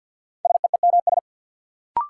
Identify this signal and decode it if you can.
Morse code, heard as a keyed tone, about 48 wpm, 700 Hz. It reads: SEEMS
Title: Sets of 3 Words - New Word Drill - 50wpm